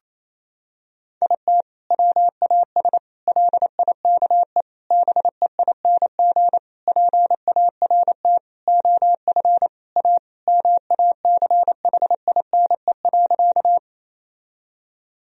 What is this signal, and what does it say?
Morse code, heard as a keyed tone, about 28 wpm, 700 Hz. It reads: IT WAS LIKE BEING PART OF A MACHINE.
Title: Sentences from Top 1000 Words - Review Part 3 - 28wpm